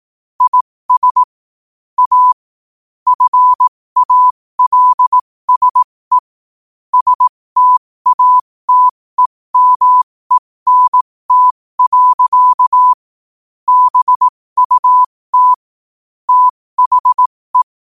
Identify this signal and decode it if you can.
Morse code, heard as a keyed tone, about 18 wpm, spaced as for 15 wpm, 1000 Hz. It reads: IS A FALSE STATEMENT. BUT THE
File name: qotdfiles / 15wpm